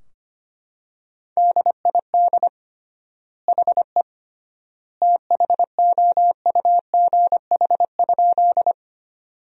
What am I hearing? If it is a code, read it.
Morse code, heard as a keyed tone, about 25 wpm, 700 Hz. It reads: DID HE THOUGH?
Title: 3x Word-Spacing - Sentences from Top 400 Words - 25wpm